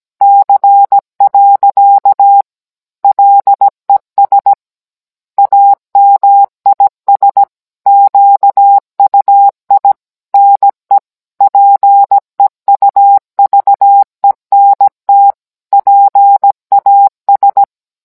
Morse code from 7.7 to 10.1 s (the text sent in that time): QUI